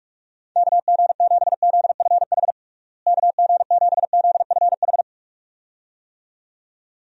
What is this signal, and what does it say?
Morse code, heard as a keyed tone, about 41 wpm, 700 Hz. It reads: KG7ZFH KG7ZFH
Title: USA Call Signs - 6 Characters - Repeated 2x - 45wpm